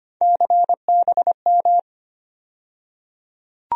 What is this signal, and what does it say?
Morse code, heard as a keyed tone, about 25 wpm, 700 Hz. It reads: CBM
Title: USA Call Sign Postfixes - 3 Letters 25wpm